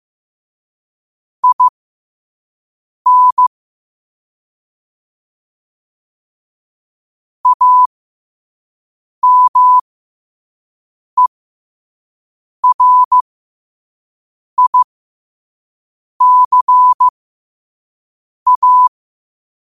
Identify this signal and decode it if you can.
Morse code, heard as a keyed tone, about 15 wpm, spaced as for 5 wpm, 1000 Hz. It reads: IN AMERICA
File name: qotdfiles / 5wpm